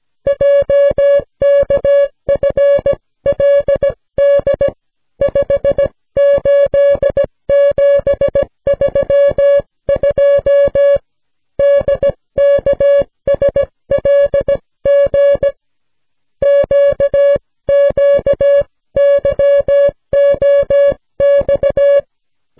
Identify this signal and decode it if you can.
Morse code, heard as a keyed tone, about 17 wpm, 560 Hz. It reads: JKFLD 58732 DKSLG QQYOX